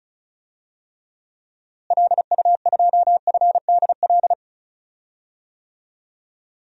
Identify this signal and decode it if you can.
Morse code, heard as a keyed tone, about 34 wpm, 700 Hz. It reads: LU2FDL
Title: DXCC-100 Call Signs - 6 Characters - Repeated 3x - 35wpm